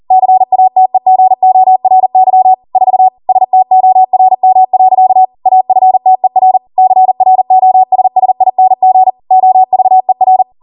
Morse code, heard as a keyed tone, about 40 wpm, 750 Hz. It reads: CATEGORY 4 STORM. AFTER CROSSING OVER